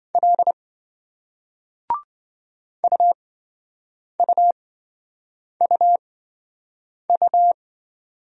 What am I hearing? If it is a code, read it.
Morse code, heard as a keyed tone, about 27 wpm, spaced as for 7 wpm, 700 Hz. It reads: L UUUU